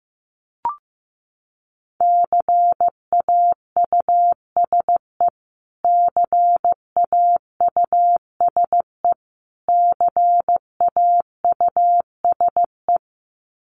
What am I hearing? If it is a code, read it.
Morse code, heard as a keyed tone, about 15 wpm, 700 Hz. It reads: CAUSE CAUSE CAUSE
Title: Top 500 Words Repeated 15wpm